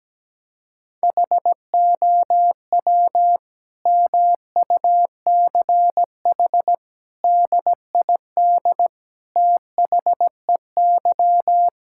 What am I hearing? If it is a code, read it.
Morse code, heard as a keyed tone, about 17 wpm, 700 Hz. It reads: HOW MUCH DID THEY
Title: Sentences from Top 1000 Words - Review Part 5 - 17wpm